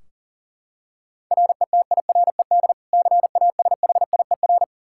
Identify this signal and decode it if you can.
Morse code, heard as a keyed tone, about 37 wpm, 700 Hz. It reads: RETIRED CASHIER
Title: Practice QSO - Exchange 3 - More Info - 40wpm